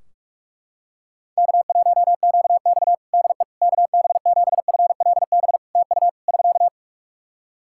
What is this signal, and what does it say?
Morse code, heard as a keyed tone, about 44 wpm, 700 Hz. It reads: K1QX DE KB7FLB TU <SK>
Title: Practice QSO - Exchange 4 - Ending - 45wpm